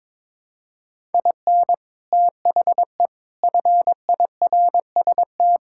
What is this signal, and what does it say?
Morse code, heard as a keyed tone, about 22 wpm, 700 Hz. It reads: IN THE FIRST
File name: practice / 22wpm / Sets of 8 Words - 22wpm